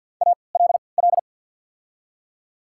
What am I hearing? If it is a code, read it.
Morse code, heard as a keyed tone, about 48 wpm, spaced as for 28 wpm, 700 Hz. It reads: ALL
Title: 3x Character Spacing - Top 100 Words - 50wpm